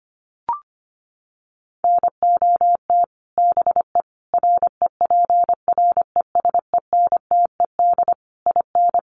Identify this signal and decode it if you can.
Morse code, heard as a keyed tone, about 25 wpm, 700 Hz. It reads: NOT BE REPRESENTED IN
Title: Sets of 5 Words 25wpm